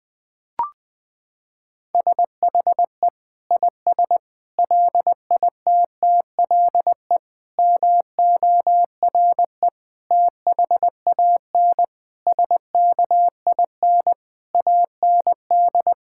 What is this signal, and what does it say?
Morse code, heard as a keyed tone, about 20 wpm, 700 Hz. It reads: SHE IS LITTLE MORE THAN SKIN AND
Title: Sentences from Top 1600 Words 20wpm